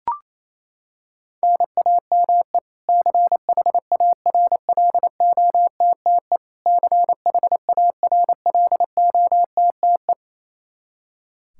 Morse code, heard as a keyed tone, about 28 wpm, 700 Hz. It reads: NAME CHARLOTTE CHARLOTTE
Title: Practice QSO - Exchange 1 - Beginning - 28wpm